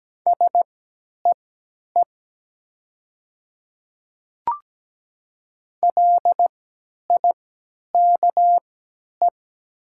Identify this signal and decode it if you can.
Morse code, heard as a keyed tone, about 17 wpm, spaced as for 10 wpm, 700 Hz. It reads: SEE LIKE
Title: 3x Character Spacing - Top 100 Words - 17wpm